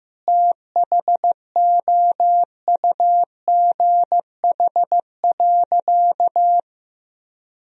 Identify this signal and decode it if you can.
Morse code, heard as a keyed tone, about 16 wpm, 700 Hz. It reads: THOUGH.
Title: Sentences from Top 1000 Words - Review Part 2 - 15wpm